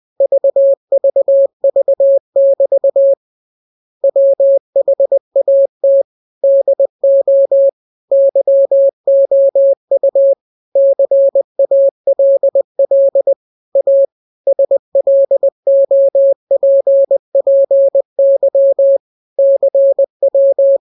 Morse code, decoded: VVV<BT> WHAT DO YOU CALL A SLOPPY CW